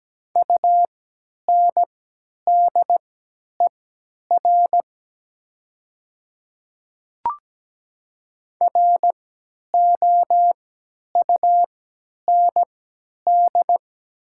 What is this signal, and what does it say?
Morse code, heard as a keyed tone, about 17 wpm, spaced as for 10 wpm, 700 Hz. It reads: UNDER ROUND
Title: 3x Character Spacing - Top 200 Words - 17wpm